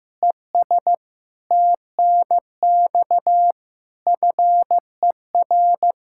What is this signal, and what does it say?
Morse code, heard as a keyed tone, about 15 wpm, 700 Hz. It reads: ES TNX FER
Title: K1USN SST QSO Practice 15wpm